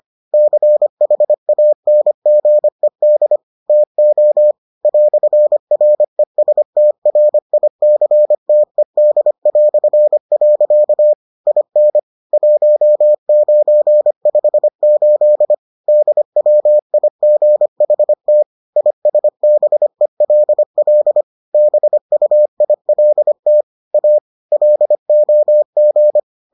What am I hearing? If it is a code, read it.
Morse code, heard as a keyed tone, about 25 wpm, 600 Hz. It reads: CHANGED TO "RESTRICTED". IN 1958 DWIGHT ISBELL BUILT A LOG